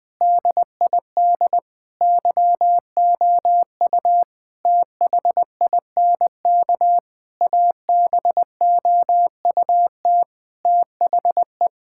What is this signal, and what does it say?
Morse code, heard as a keyed tone, about 20 wpm, 700 Hz. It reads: DID YOU THINK ABOUT THE
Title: Sentences from Top 1000 Words - Review Part 5 - 20wpm